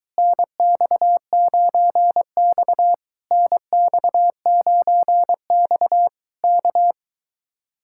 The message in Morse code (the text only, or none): NX9X NX9X K